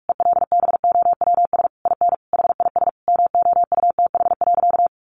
Morse code, heard as a keyed tone, about 46 wpm, 700 Hz. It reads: ELBOWS IN HIS MOUTH.
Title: A Sound Of Thunder 45wpm